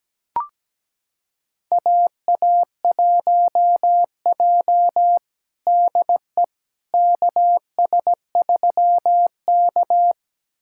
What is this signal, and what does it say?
Morse code, heard as a keyed tone, about 17 wpm, 700 Hz. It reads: AA1J DE KS3K
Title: USA Callsign DE Callsign - 4 Characters - 17wpm